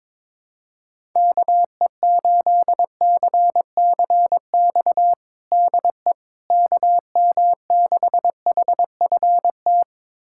Morse code, heard as a keyed tone, about 22 wpm, 700 Hz. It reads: KE8CCX DE KM6HFT
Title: USA Callsign DE Callsign - 6 Characters - 22wpm